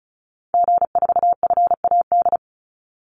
Morse code, heard as a keyed tone, about 34 wpm, 700 Hz. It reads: G4FAD
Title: Call Signs Repeated 2x - 35wpm